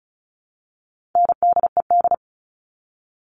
Morse code, heard as a keyed tone, about 34 wpm, 700 Hz. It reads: NDED
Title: All Cut Numbers - 4-Digit 35wpm